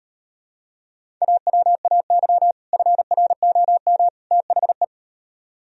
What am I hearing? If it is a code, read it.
Morse code, heard as a keyed tone, about 37 wpm, 700 Hz. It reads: AWAY FROM THE